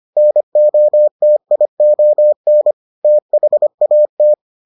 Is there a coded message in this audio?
Yes: Morse code from 0.2 to 4.3 s, heard as a keyed tone, about 25 wpm, 600 Hz. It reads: NOTION THAT